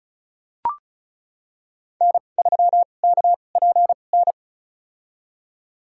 Morse code, heard as a keyed tone, about 36 wpm, spaced as for 25 wpm, 700 Hz. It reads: N3KPN
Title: USA Call Signs - 5 Characters - 2x Character Spacing - 35wpm